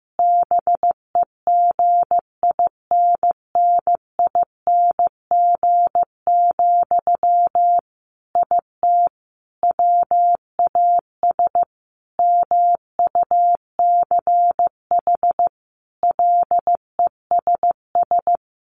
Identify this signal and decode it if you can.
Morse code, heard as a keyed tone, about 15 wpm, 700 Hz. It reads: BEGINNING, IT WAS MUCH LESS
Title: Sentences from Top 1400 Words 15wpm